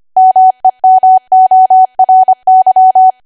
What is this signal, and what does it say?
Morse code, heard as a keyed tone, about 25 wpm, 750 Hz. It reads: MEMORY
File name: cwa / W1AW-Level3 / 25WPM